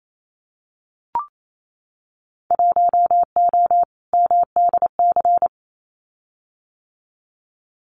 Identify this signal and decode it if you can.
Morse code, heard as a keyed tone, about 28 wpm, 700 Hz. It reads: 1O MDC
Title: Winter Field Day Exchanges 28wpm